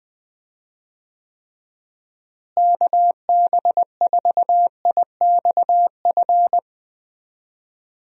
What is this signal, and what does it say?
Morse code, heard as a keyed tone, about 21 wpm, 700 Hz. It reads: KB4IXF
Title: USA Call Signs - 6 Characters - Repeated 3x - 20wpm